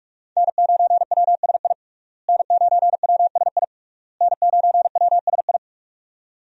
Morse code, heard as a keyed tone, about 46 wpm, 700 Hz. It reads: N9WSI N9WSI N9WSI
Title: USA Call Signs - 5 Characters - Repeated 3x - 45wpm